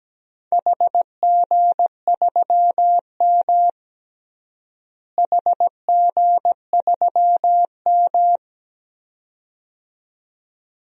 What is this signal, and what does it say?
Morse code, heard as a keyed tone, about 17 wpm, 700 Hz. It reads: HG3M HG3M
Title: DXCC-100 Call Signs - 4 Characters - Repeated 2x - 17wpm